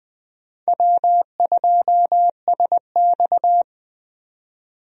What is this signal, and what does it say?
Morse code, heard as a keyed tone, about 20 wpm, 700 Hz. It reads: W2SX
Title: USA Call Signs - 4 Characters - 20wpm